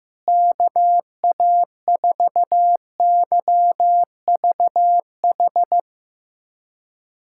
Morse code, heard as a keyed tone, about 15 wpm, 700 Hz. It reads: KA4YVH
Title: USA Callsign DE Callsign 15wpm